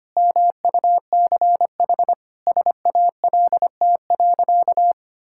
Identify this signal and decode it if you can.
Morse code, heard as a keyed tone, about 25 wpm, 700 Hz. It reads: MUCH SALT.
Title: Sentences from Top 1000 Words - Review Part 5 - 25wpm